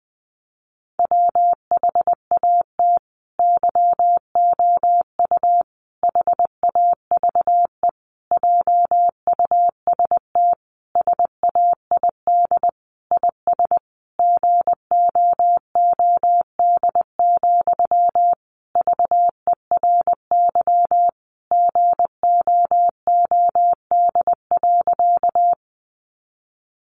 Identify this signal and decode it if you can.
Morse code, heard as a keyed tone, about 20 wpm, 700 Hz. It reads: WHAT YOU HAVE JUST SAID IS GOOD, VERY GOOD.
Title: Sentences from Top 200 Words 20wpm